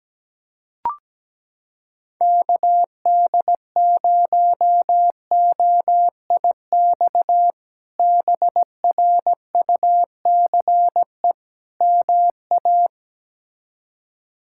Morse code, heard as a keyed tone, about 17 wpm, 700 Hz. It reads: KD0OIX BRUCE MA
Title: K1USN SST QSO Practice - 17wpm